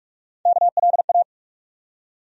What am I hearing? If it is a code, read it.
Morse code, heard as a keyed tone, about 46 wpm, 700 Hz. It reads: KLA
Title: USA Call Sign Postfixes - 3 Letters 45wpm